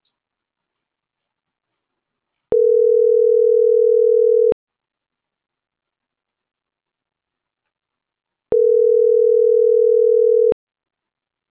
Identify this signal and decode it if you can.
Ringback tone (North America), 440 + 480 Hz.